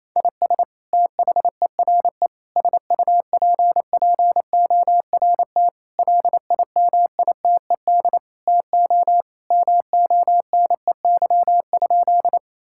Morse code, heard as a keyed tone, about 28 wpm, 700 Hz. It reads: IS THERE SUPPORT LIMITED TO MONEY?
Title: Sentences from Top 1000 Words - Review Part 6 - 28wpm